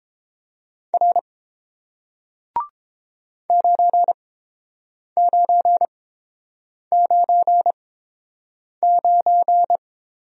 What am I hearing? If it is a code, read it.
Morse code, heard as a keyed tone, about 30 wpm, spaced as for 7 wpm, 700 Hz. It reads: R 9999